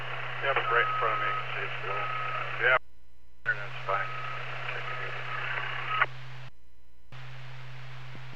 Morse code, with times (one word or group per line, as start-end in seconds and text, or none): none